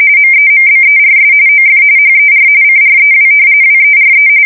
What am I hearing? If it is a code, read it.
RTTY, about 45.45 baud. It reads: MEMBERS OF HER CREW WERE
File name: fsk170 / battleshipnj